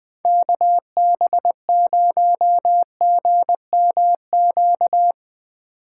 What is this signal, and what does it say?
Morse code, heard as a keyed tone, about 20 wpm, 700 Hz. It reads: KB0GMQ
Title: USA Call Signs - 2x Character Spacing - 20wpm